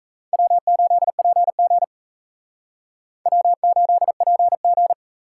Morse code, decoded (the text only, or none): W8PG W8PG